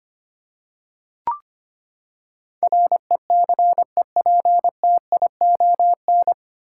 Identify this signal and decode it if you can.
Morse code, heard as a keyed tone, about 25 wpm, 700 Hz. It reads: RECEPTION
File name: Nine Letter Words - 25wpm